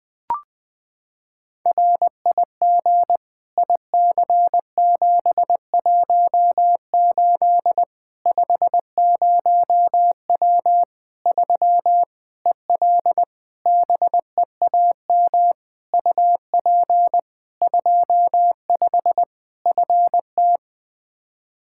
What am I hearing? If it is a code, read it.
Morse code, heard as a keyed tone, about 21 wpm, 700 Hz. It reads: RIG IC718 50W 3 EL BEAM UP 25 FT
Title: Practice QSO - Exchange 2 - Info - 20wpm